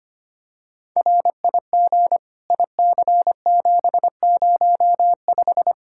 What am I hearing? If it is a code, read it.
Morse code, heard as a keyed tone, about 25 wpm, 700 Hz. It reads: RIG IC705
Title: Practice QSO - Exchange 2 - Info - 25wpm